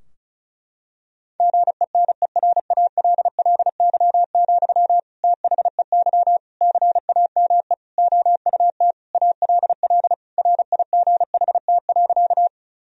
Morse code, heard as a keyed tone, about 34 wpm, 700 Hz. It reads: GENERALLY, THEY CAME OUT ALL RIGHT.
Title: Sentences from Top 1000 Words - Review Part 4 - 35wpm